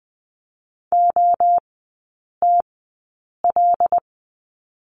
Morse code, heard as a keyed tone, about 20 wpm, spaced as for 8 wpm, 700 Hz. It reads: OTL
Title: 093 Lesson 16e - Letter c - Three Characters 20wpm